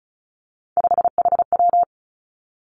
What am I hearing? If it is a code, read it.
Morse code, heard as a keyed tone, about 36 wpm, 700 Hz. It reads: 5HW
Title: Three Challenging Letter-Number Combinations 35wpm